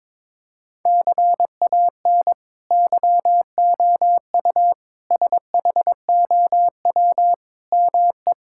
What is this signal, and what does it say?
Morse code, heard as a keyed tone, about 22 wpm, 700 Hz. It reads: CAN YOU SHOW ME
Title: Sentences from Top 1000 Words - Review Part 6 - 22wpm